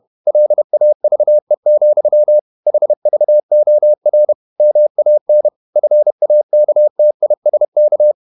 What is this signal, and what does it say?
Morse code, heard as a keyed tone, about 31 wpm, 600 Hz. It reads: LAVE, HVOR MAN FAKTISK
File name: Ferie QRQ 31wpm